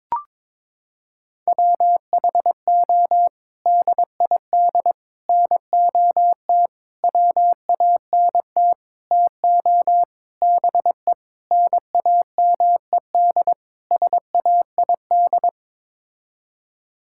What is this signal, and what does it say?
Morse code, heard as a keyed tone, about 22 wpm, 700 Hz. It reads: WHO DID NOT WANT TO BE NAMED SAID
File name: practice / 22wpm / Sets of 8 Words - 22wpm